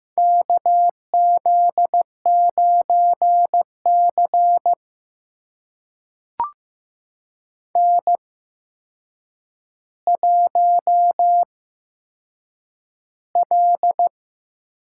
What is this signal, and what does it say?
Morse code, heard as a keyed tone, about 15 wpm, spaced as for 7 wpm, 700 Hz. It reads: KZ9C N1L
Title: USA Call Signs - 4 Characters - 8x Character Spacing - 15wpm